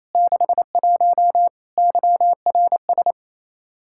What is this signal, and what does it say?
Morse code, heard as a keyed tone, about 27 wpm, 700 Hz. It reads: 61 YRS
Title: Practice QSO - Exchange 3 - More Info - 28wpm